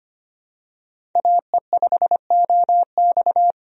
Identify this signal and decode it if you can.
Morse code, heard as a keyed tone, about 25 wpm, 700 Hz. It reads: AE5OX
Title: USA Call Signs - 5 Characters - 25wpm